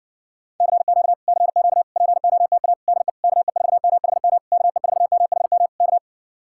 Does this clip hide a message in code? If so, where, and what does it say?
Yes: Morse code from 0.6 to 6.0 s, heard as a keyed tone, about 60 wpm, 700 Hz. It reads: CQ CQ POTA DE KE4MHM KE4MHM K